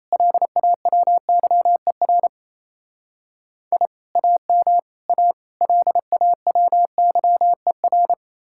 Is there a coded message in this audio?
Yes: Morse code from 0.1 to 8.1 s, heard as a keyed tone, about 36 wpm, 700 Hz. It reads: LAWYER I AM A LAWYER